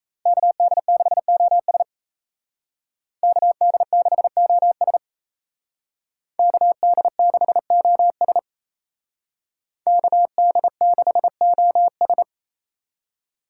Morse code, heard as a keyed tone, about 37 wpm, 700 Hz. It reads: KD6OS KD6OS KD6OS KD6OS